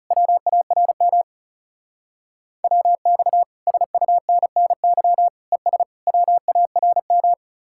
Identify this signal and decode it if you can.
Morse code, heard as a keyed tone, about 35 wpm, 700 Hz. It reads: WARM WX SUNNY ES WARM
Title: Speed-Racing Common QSO Phrases Encore - 30wpm